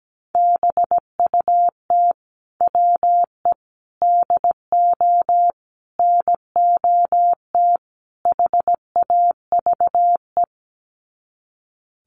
Morse code, heard as a keyed tone, about 17 wpm, 700 Hz. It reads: BUT WE DO NOT HAVE
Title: Sets of 5 Words Encore2 17wpm